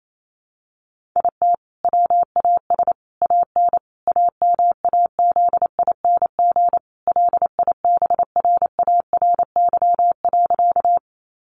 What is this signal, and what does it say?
Morse code, heard as a keyed tone, about 28 wpm, 700 Hz. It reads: IT WAS AN AMAZING LIBRARY.
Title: Sentences from Top 1500 Words 28wpm